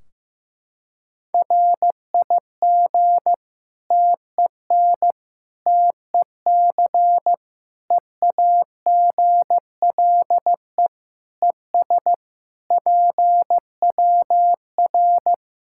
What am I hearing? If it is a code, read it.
Morse code, heard as a keyed tone, about 15 wpm, 700 Hz. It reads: RIG TEN TEC EAGLE ES PWR
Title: Practice QSO - Exchange 2 - Info - 15wpm